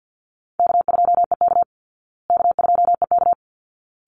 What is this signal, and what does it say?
Morse code, heard as a keyed tone, about 50 wpm, 700 Hz. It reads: K2EK K2EK
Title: Easy Call Signs Repeated 2x - 50wpm